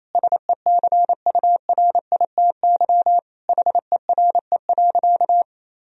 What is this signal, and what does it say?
Morse code, heard as a keyed tone, about 28 wpm, 700 Hz. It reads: SECURITY HERE.
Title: Sentences from Top 1000 Words - Review Part 4 - 28wpm